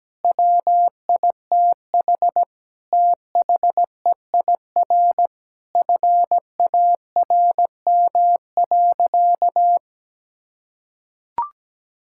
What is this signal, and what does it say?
Morse code, heard as a keyed tone, about 17 wpm, 700 Hz. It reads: WITH THEIR FARM.
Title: Sentences from Top 1000 Words - Review Part 6 - 17wpm